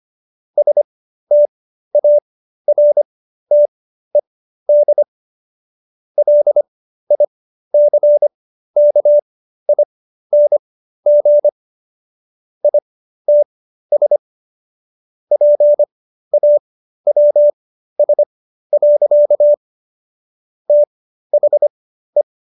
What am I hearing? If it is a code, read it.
Morse code, heard as a keyed tone, about 25 wpm, spaced as for 13 wpm, 590 Hz. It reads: STARTED LICKING ITS PAWS. THE